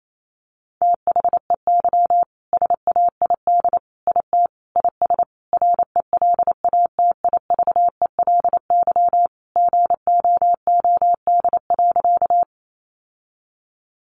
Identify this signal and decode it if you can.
Morse code, heard as a keyed tone, about 28 wpm, 700 Hz. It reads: THEY SAID IT IS RELATIVELY GOOD.
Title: Sentences from Top 1100 Words 28wpm